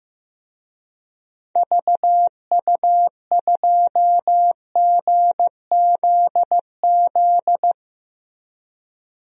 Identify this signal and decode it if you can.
Morse code, heard as a keyed tone, about 16 wpm, 700 Hz. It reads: VU2GZZ
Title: DXCC-100 Call Signs - 6 Characters - Repeated 3x - 15wpm